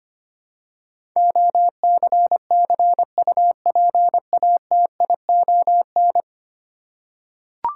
Morse code, heard as a keyed tone, about 25 wpm, 700 Hz. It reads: OCCUPATION
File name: Ten Letter Words - 25wpm